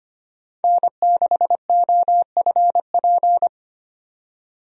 Morse code, heard as a keyed tone, about 25 wpm, 700 Hz. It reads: N6OFP
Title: USA Call Signs - 5 Characters - 25wpm